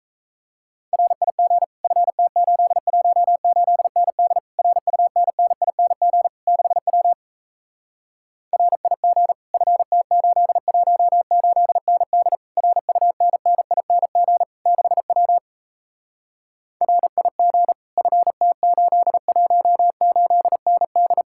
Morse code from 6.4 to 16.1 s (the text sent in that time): 6W RIG FT818ND RUNNING 6W